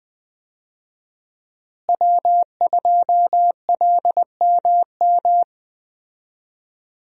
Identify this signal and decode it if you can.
Morse code, heard as a keyed tone, about 21 wpm, 700 Hz. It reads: W2LMM
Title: USA Call Signs - 5 Characters - Repeated 2x - 20wpm